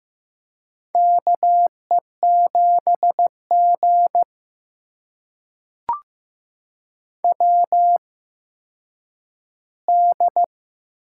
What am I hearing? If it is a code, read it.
Morse code, heard as a keyed tone, about 15 wpm, spaced as for 9 wpm, 700 Hz. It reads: KE7G WD
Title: USA Call Signs - 4 Characters - 8x Character Spacing - 15wpm